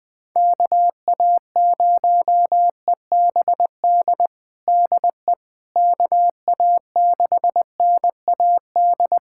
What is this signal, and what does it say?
Morse code, heard as a keyed tone, about 20 wpm, 700 Hz. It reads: KA0EBD DE KA6NAD
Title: USA Callsign DE Callsign - 6 Characters - 20wpm